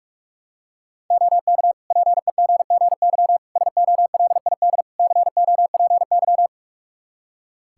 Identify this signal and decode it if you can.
Morse code, heard as a keyed tone, about 43 wpm, 700 Hz. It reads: OK PEGGY SOLID COPY
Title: Practice QSO - Exchange 3 - More Info - 45wpm